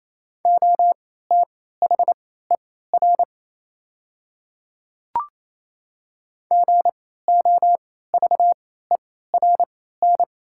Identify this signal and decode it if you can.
Morse code, heard as a keyed tone, about 27 wpm, spaced as for 16 wpm, 690 Hz. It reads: OTHER GOVERN
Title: 3x Character Spacing - Top 500 Words - 28wpm